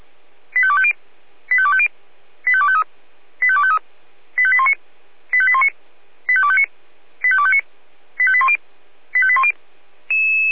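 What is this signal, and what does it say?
Five-tone selective call (ZVEI), heard as quick runs of five tones, tones of about 70 ms. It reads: 85250 85250 85253 85253 85819 85819 85259 85259 85810 85810
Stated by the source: Five-tone ZVEI selective call 85232, received by SDR.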